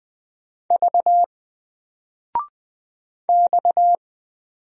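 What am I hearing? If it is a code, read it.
Morse code, heard as a keyed tone, about 20 wpm, 700 Hz. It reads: V X
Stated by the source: Single letter and number Morse drill at 20 wpm, from N4LCD.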